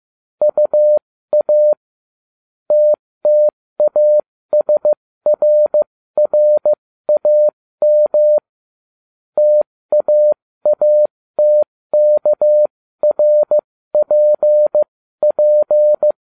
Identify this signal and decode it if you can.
Morse code, heard as a keyed tone, about 15 wpm, spaced as for 13 wpm, 600 Hz. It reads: UA TTASRRAM TAATKRPP